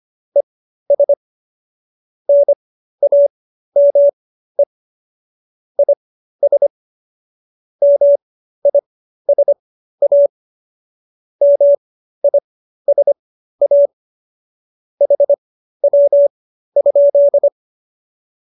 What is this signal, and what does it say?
Morse code, heard as a keyed tone, about 25 wpm, spaced as for 13 wpm, 570 Hz. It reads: ES NAME IS MISA MISA HW?